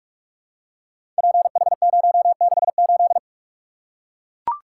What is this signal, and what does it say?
Morse code, heard as a keyed tone, about 44 wpm, 700 Hz. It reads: PH068
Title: Code Groups - 5 Letter:Numbers - 45wpm